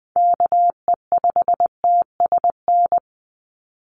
Morse code, heard as a keyed tone, about 20 wpm, 700 Hz. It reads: KE5TSN
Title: USA Call Signs 20wpm